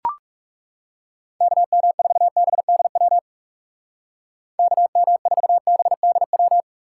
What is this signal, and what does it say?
Morse code, heard as a keyed tone, about 36 wpm, 700 Hz. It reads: KM4BDW KM4BDW
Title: Speed Racing - USA Call Signs 30wpm